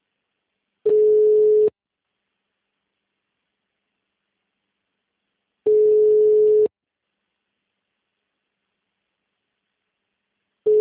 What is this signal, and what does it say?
Ringback tone (Europe, 425 Hz).